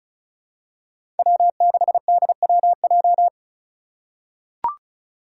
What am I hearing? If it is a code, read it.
Morse code, heard as a keyed tone, about 34 wpm, 700 Hz. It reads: W6DWJ
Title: Five Challenging Letter-Number Combinations 35wpm